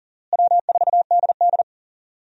Morse code, heard as a keyed tone, about 40 wpm, 700 Hz. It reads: W4DD
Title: USA Call Signs - 4 Characters - Repeated 3x - 40wpm